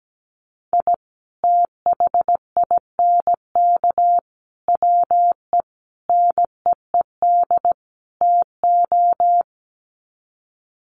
Morse code, heard as a keyed tone, about 17 wpm, 700 Hz. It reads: I THINK WE NEED TO
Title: Sets of 5 Words 17wpm